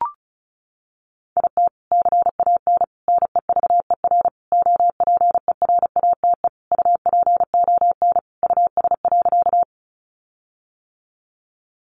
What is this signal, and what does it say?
Morse code, heard as a keyed tone, about 34 wpm, 700 Hz. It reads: IT CAN NEVER OPERATE UPON US.
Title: Sentences from Top 1500 Words 35wpm